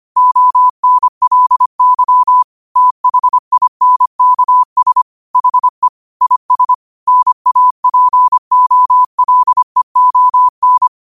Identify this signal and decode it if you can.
Morse code, heard as a keyed tone, about 25 wpm, 1000 Hz. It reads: ONLY THINKS HE IS NAPOLEON
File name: qotdfiles / 25wpm